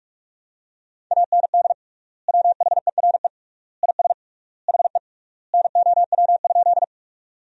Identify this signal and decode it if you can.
Morse code, heard as a keyed tone, about 42 wpm, 700 Hz. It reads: AND WHERE IS HE NOW?
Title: 3x Word-Spacing - Sentences from Top 200 Words - 45wpm